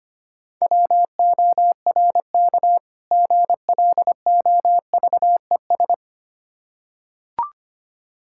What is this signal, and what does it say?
Morse code, heard as a keyed tone, about 25 wpm, 700 Hz. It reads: WORK GLOVES
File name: practice / 25wpm / Difficult Words with Hints - 25wpm